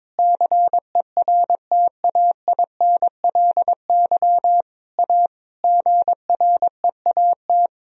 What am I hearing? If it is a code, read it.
Morse code, heard as a keyed tone, about 22 wpm, 700 Hz. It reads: CERTAINLY A GREAT
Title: Sentences from Top 1000 Words - Review Part 1 - 22wpm